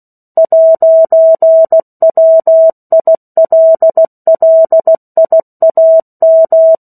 Morse code, heard as a keyed tone, about 16 wpm, 650 Hz. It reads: 'WILLIAM
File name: 03-Chapter-03 16 WPM